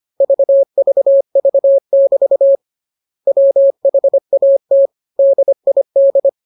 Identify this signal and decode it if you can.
Morse code, heard as a keyed tone, about 25 wpm, 550 Hz. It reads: VVV<BT> WHAT DID
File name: jokes / joke 2 25 0000